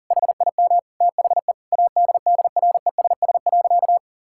Morse code, heard as a keyed tone, about 37 wpm, 700 Hz. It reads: HIM THE ADDRESS.
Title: Sentences from Top 1000 Words - Review Part 6 - 40wpm